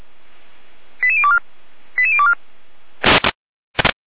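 Five-tone selective call (ZVEI), heard as quick runs of five tones, tones of about 70 ms. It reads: 88024 88024
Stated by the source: ZVEI five-tone selective call received on SDR.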